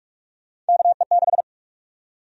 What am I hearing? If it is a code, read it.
Morse code, heard as a keyed tone, about 46 wpm, 700 Hz. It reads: KE6
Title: USA Call Sign Prefixes - 45wpm